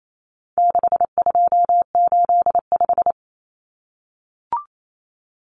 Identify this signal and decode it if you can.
Morse code, heard as a keyed tone, about 28 wpm, 700 Hz. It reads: 6285
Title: Common Cut Numbers - 4-Digit 28wpm